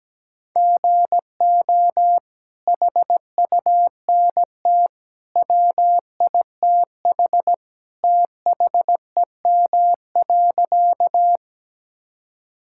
Morse code, heard as a keyed tone, about 17 wpm, 700 Hz. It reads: GO HUNT WITH THEM.
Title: Sentences from Top 1600 Words 17wpm